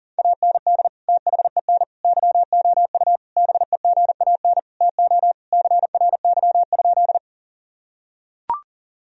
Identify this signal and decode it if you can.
Morse code, heard as a keyed tone, about 40 wpm, 700 Hz. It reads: AND THEN YOU BEGAN TO CRY?
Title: Sentences from Top 1000 Words - Review Part 2 - 40wpm